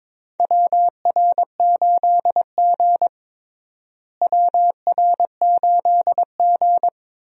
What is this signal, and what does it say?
Morse code, heard as a keyed tone, about 22 wpm, 700 Hz. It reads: WR8G WR8G
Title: USA Call Signs - 4 Characters - Repeated 2x - 22wpm